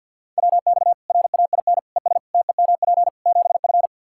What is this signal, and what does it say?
Morse code, heard as a keyed tone, about 48 wpm, 700 Hz. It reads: WX RAIN ES TEMP 7F
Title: Practice QSO - Exchange 2 - Info - 50wpm